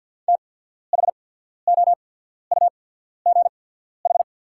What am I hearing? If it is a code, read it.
Morse code, heard as a keyed tone, about 48 wpm, spaced as for 14 wpm, 700 Hz. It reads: THOUGH
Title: 8x Character Spacing - Top 500 Words - 50wpm